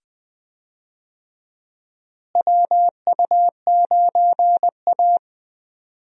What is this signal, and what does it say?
Morse code, heard as a keyed tone, about 20 wpm, 700 Hz. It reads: WU9A
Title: USA Call Signs - 4 Characters - Repeated 2x - 20wpm